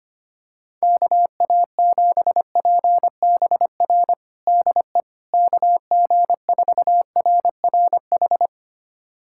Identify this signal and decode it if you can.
Morse code, heard as a keyed tone, about 25 wpm, 700 Hz. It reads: KA7PBR DE KG4RRH
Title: USA Callsign DE Callsign - 6 Characters - 25wpm